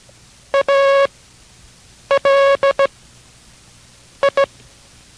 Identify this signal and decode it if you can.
Morse code, heard as a keyed tone, about 17 wpm, spaced as for 6 wpm, 550 Hz. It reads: ALI